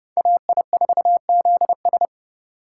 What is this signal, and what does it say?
Morse code, heard as a keyed tone, about 29 wpm, 700 Hz. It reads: AI4ZS
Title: Speed Racing - USA 2x2 Call Signs 25wpm